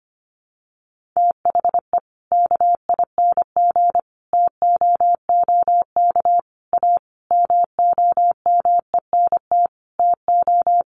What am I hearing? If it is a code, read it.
Morse code, heard as a keyed tone, about 25 wpm, 700 Hz. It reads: THE KING TOOK A MOMENT TO
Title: Sentences from Top 1500 Words 25wpm